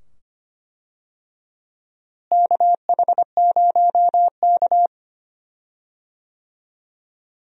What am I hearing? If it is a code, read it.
Morse code, heard as a keyed tone, about 25 wpm, 700 Hz. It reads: KH0K
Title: USA Call Signs - 4 Characters - Repeated 2x - 25wpm